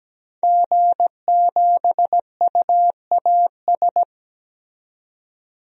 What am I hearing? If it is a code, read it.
Morse code, heard as a keyed tone, about 17 wpm, 700 Hz. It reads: G7UAS
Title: DXCC-100 Call Signs - 5 Characters - 17wpm